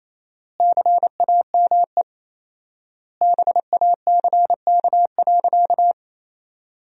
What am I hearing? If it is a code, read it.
Morse code, heard as a keyed tone, about 28 wpm, 700 Hz. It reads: CAME BACK.
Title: 4x Word-Spacing - Sentences from Top 300 Words - 28wpm